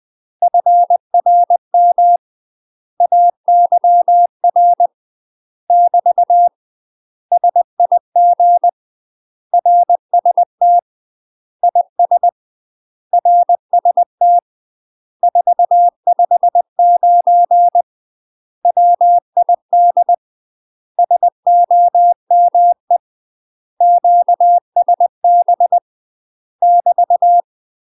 Morse code, heard as a keyed tone, about 20 wpm, 700 Hz. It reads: FRM AYR <BT> SIG RST IS RST 459 WID SOME QSB <BT>
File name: UK MASTER QSOs TRAINING / 20WPM / UK MASTER QSOs 0024